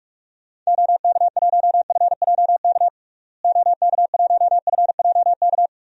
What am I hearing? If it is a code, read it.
Morse code, heard as a keyed tone, about 46 wpm, 700 Hz. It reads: OK1FJK OK1FJK
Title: DXCC-100 Call Signs - 6 Characters - Repeated 3x - 45wpm